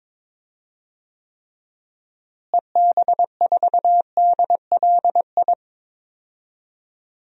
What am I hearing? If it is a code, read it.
Morse code, heard as a keyed tone, about 22 wpm, 700 Hz. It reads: EB4DLI
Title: DXCC-100 Call Signs - 6 Characters - Repeated 2x - 22wpm